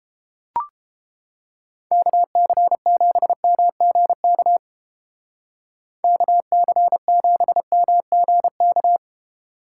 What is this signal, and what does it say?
Morse code, heard as a keyed tone, about 29 wpm, 700 Hz. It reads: KC7MGK KC7MGK